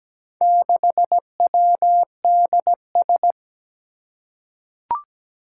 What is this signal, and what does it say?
Morse code, heard as a keyed tone, about 17 wpm, 700 Hz. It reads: 6WDS
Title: Four Challenging Letter-Number Combinations 17wpm